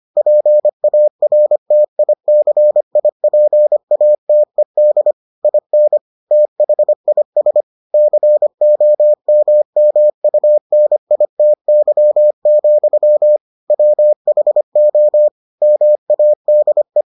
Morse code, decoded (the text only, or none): PARTICIPATED IN THIS COMMUNITY, WHO MADE